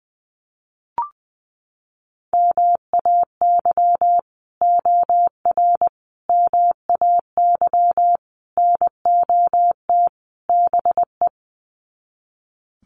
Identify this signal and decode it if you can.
Morse code, heard as a keyed tone, about 20 wpm, 700 Hz. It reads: MAY OR MAY NOT BE
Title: Sets of 5 Words 20wpm